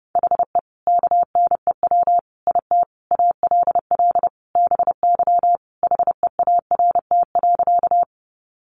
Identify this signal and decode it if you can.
Morse code, heard as a keyed tone, about 30 wpm, 700 Hz. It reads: HE KNEW IT ALL BY HEART.
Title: Sentences from Top 1200 Words 30wpm